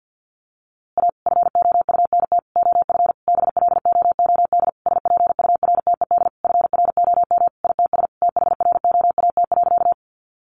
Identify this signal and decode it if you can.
Morse code, heard as a keyed tone, about 49 wpm, 700 Hz. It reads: A FOUNT OF BLOOD SPURTED FROM ITS THROAT.
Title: A Sound Of Thunder 50wpm